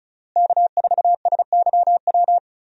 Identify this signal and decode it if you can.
Morse code, heard as a keyed tone, about 34 wpm, 700 Hz. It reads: K4SYW
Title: USA Call Signs - 5 Characters - Repeated 3x - 35wpm